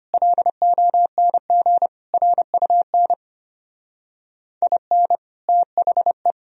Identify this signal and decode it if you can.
Morse code, heard as a keyed tone, about 31 wpm, 700 Hz. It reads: LONG RUN IN THE
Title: Speed-Racing - Sets of 4 Words 25wpm